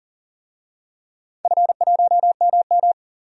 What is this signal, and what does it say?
Morse code, heard as a keyed tone, about 38 wpm, 700 Hz. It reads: F1MM
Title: DXCC-100 Call Signs - 4 Characters - 40wpm